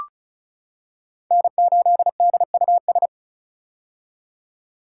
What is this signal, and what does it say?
Morse code, heard as a keyed tone, about 34 wpm, 700 Hz. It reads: N8DUS
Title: K1USN SST QSO Practice - 35wpm